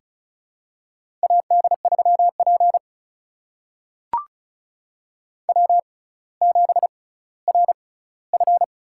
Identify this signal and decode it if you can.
Morse code, heard as a keyed tone, about 36 wpm, spaced as for 18 wpm, 700 Hz. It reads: AD3P W7RF
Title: USA Call Signs - 4 Characters - 6x Character Spacing - 35wpm